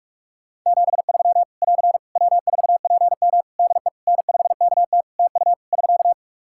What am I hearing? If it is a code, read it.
Morse code, heard as a keyed tone, about 41 wpm, 700 Hz. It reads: 73 <AR> W4PM DE N5KT TU <SK>
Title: K1USN SST QSO Practice 45wpm